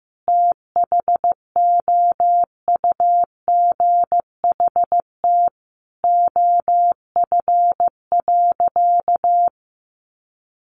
Morse code, heard as a keyed tone, about 15 wpm, 700 Hz. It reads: THOUGHT OF.
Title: Sentences from Top 300 Words 15wpm